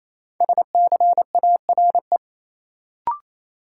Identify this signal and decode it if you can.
Morse code, heard as a keyed tone, about 28 wpm, 700 Hz. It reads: SCARE
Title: Sentences from Top 1000 Words - Review Part 5 - New Word Drill - 28wpm